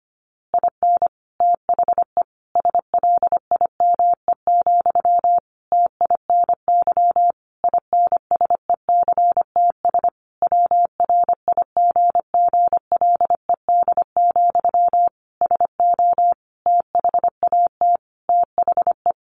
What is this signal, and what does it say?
Morse code, heard as a keyed tone, about 25 wpm, 700 Hz. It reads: IN THE SLIME, TINY INSECTS WRIGGLED, SO THAT THE
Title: A Sound Of Thunder 25wpm